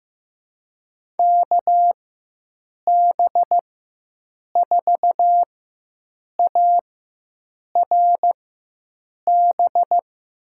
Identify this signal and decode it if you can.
Morse code, heard as a keyed tone, about 15 wpm, spaced as for 7 wpm, 700 Hz. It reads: KB4ARB
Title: USA Call Signs - 4x Character Spacing - 15wpm